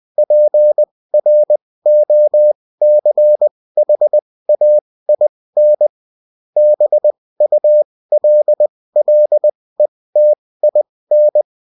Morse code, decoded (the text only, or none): PROCHAIN BULLETIN